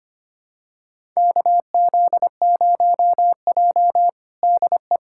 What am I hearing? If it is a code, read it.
Morse code, heard as a keyed tone, about 25 wpm, 700 Hz. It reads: KZ0J DE
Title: USA Callsign DE Callsign - 4 Characters - 25wpm